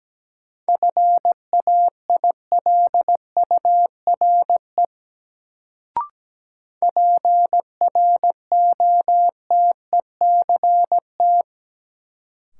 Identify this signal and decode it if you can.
Morse code, heard as a keyed tone, about 17 wpm, 700 Hz. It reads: FAILURE PROTECT
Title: Sentences from Top 900 Words - New Word Drill - 17wpm